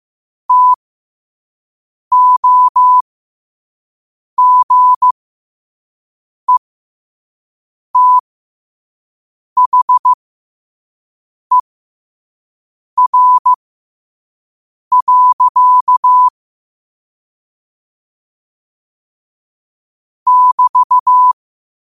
Morse code, heard as a keyed tone, about 15 wpm, spaced as for 5 wpm, 1000 Hz. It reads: TOGETHER. <BT>
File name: 5wpm